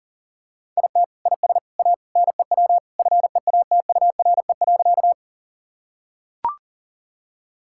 Morse code, heard as a keyed tone, about 40 wpm, 700 Hz. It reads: IT IS A NEW FEATURE.
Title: Sentences from Top 1000 Words - Review Part 5 - 40wpm